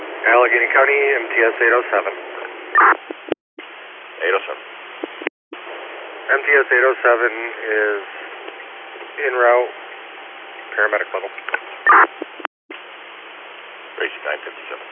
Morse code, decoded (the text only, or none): none